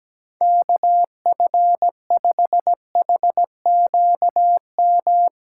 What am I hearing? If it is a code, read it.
Morse code, heard as a keyed tone, about 18 wpm, 700 Hz. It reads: KF5HQM
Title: Speed Racing - USA 2x3 Call Signs 17wpm